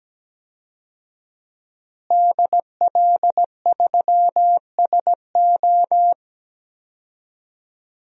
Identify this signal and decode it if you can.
Morse code, heard as a keyed tone, about 17 wpm, 700 Hz. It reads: DL3SO
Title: DXCC-100 Call Signs - 5 Characters - Repeated 3x - 17wpm